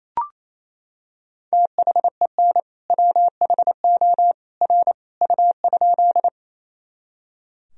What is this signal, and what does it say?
Morse code, heard as a keyed tone, about 28 wpm, 700 Hz. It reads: THEN WHO R U?
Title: Sentences from Top 100 Words and Abbreviations 28wpm